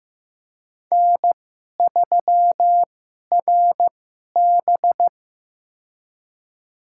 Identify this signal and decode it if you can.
Morse code, heard as a keyed tone, about 15 wpm, spaced as for 11 wpm, 700 Hz. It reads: N3RB
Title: USA Call Signs - 4 Characters - 2x Character Spacing - 15wpm